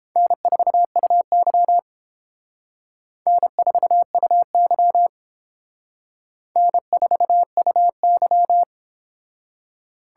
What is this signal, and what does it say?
Morse code, heard as a keyed tone, about 29 wpm, 700 Hz. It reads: N4UY N4UY N4UY